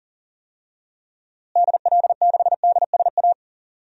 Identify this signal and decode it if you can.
Morse code, heard as a keyed tone, about 37 wpm, 700 Hz. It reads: DL6DSA
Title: DXCC-100 Call Signs - 6 Characters - Repeated 2x - 40wpm